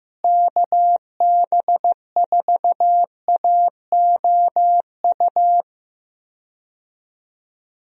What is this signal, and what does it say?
Morse code, heard as a keyed tone, about 15 wpm, 700 Hz. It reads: KB4AOU
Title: USA Call Signs - 6 Characters - Repeated 3x - 15wpm